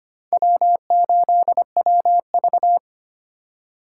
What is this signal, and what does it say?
Morse code, heard as a keyed tone, about 32 wpm, 700 Hz. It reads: W8WV